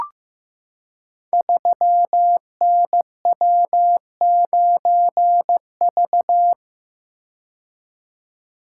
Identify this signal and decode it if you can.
Morse code, heard as a keyed tone, about 15 wpm, 700 Hz. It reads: 3NW9V
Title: Code Groups - 5 Letter:Numbers - 15wpm